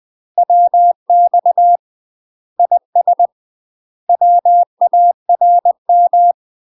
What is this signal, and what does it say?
Morse code, heard as a keyed tone, about 20 wpm, 700 Hz. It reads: WX IS WARM
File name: UK MASTER QSOs TRAINING/20WPM